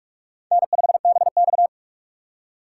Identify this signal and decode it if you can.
Morse code, heard as a keyed tone, about 44 wpm, 700 Hz. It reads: N5BX
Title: USA Call Signs - 4x Character Spacing - 45wpm